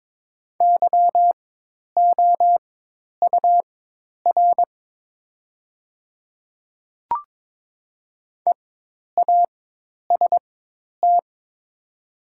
Morse code, heard as a keyed tone, about 22 wpm, spaced as for 10 wpm, 700 Hz. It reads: YOUR EAST
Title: 4x Character Spacing - Top 500 Words - 22wpm